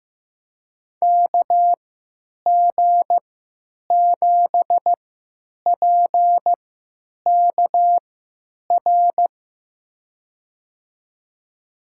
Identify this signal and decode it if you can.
Morse code, heard as a keyed tone, about 15 wpm, spaced as for 9 wpm, 700 Hz. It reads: KG7PKR
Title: USA Call Signs - 3x Character Spacing - 15wpm